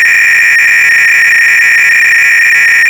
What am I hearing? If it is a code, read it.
300-baud modem data.
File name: ACR-Audio-Cassette-Programs / CharlesSwiger-Donations / GOBLINS cload-g